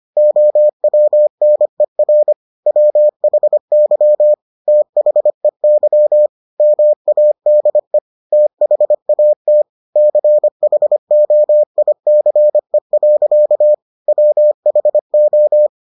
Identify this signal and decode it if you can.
Morse code, heard as a keyed tone, about 25 wpm, 600 Hz. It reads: OWNER WHY THEY MADE THAT CHOICE. WHO